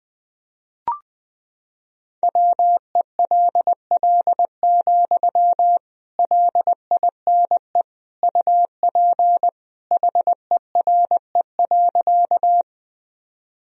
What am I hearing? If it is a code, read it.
Morse code, heard as a keyed tone, about 19 wpm, 700 Hz. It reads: WELL, LINE UP HERE.
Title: Sentences from Top 1000 Words - Review Part 1 - 20wpm